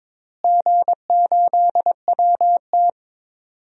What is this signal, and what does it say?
Morse code, heard as a keyed tone, about 21 wpm, 700 Hz. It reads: G8WT
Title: DXCC-100 Call Signs - 4 Characters - Repeated 3x - 22wpm